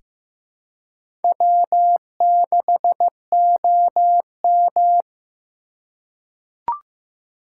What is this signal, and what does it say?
Morse code, heard as a keyed tone, about 15 wpm, 700 Hz. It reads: W6OM
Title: USA Call Signs - 4 Characters - 3x Character Spacing - 15wpm